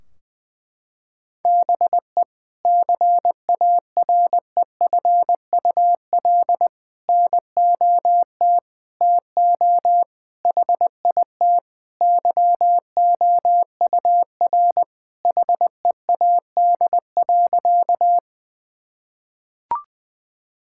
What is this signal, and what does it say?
Morse code, heard as a keyed tone, about 20 wpm, 700 Hz. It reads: BE CAREFUL NOT TO HIT YOUR HEAD.
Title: Sentences from Top 1600 Words 20wpm